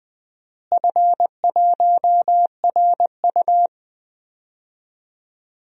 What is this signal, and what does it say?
Morse code, heard as a keyed tone, about 20 wpm, 700 Hz. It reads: F1RU
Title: DXCC-100 Call Signs - 4 Characters - Repeated 3x - 20wpm